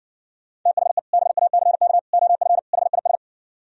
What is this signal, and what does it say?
Morse code, heard as a keyed tone, about 69 wpm, 700 Hz. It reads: THE BACK OF HIS